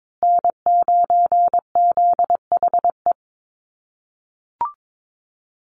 Morse code, heard as a keyed tone, about 22 wpm, 700 Hz. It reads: N9ZHE
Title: USA Call Signs Repeated 2x 22wpm